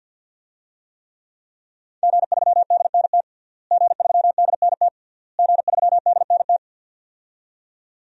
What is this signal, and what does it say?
Morse code, heard as a keyed tone, about 51 wpm, 700 Hz. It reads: G3DNT G3DNT G3DNT
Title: DXCC-100 Call Signs - 5 Characters - Repeated 3x - 50wpm